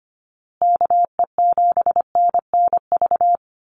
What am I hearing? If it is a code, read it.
Morse code, heard as a keyed tone, about 25 wpm, 700 Hz. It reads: KE7NNV
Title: USA Call Signs Repeated 2x 25wpm